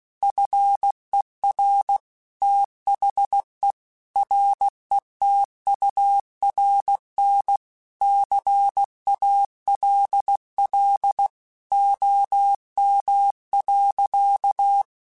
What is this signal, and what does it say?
Morse code, heard as a keyed tone, about 16 wpm, 800 Hz. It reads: FER THE RETURN CALL OM.
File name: QSOs / QSOs-mp3s / kc7haa@15